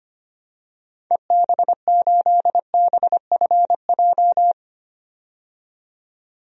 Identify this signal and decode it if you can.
Morse code, heard as a keyed tone, about 25 wpm, 700 Hz. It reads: EB8BFJ
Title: DXCC-100 Call Signs - 6 Characters - 25wpm